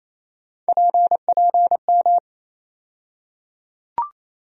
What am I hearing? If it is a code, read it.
Morse code, heard as a keyed tone, about 28 wpm, 700 Hz. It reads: PPM
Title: USA Call Sign Postfixes - 3 Letters 28wpm